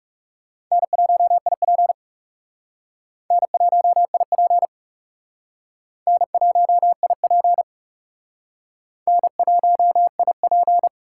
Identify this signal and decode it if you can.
Morse code, heard as a keyed tone, about 36 wpm, 700 Hz. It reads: N1IP N1IP N1IP N1IP
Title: Speed Racing - USA Call Signs 30wpm